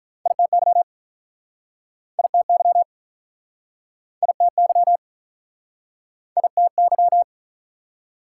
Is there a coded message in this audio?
Yes: Morse code from 0.3 to 7.2 s, heard as a keyed tone, about 46 wpm, 690 Hz. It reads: ITY ITY ITY ITY